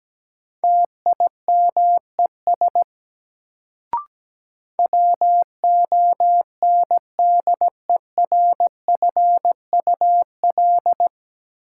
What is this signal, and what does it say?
Morse code, heard as a keyed tone, about 17 wpm, 700 Hz. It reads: TIMES WONDERFUL
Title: Sentences from Top 1100 Words - New Word Drill Alt - 17wpm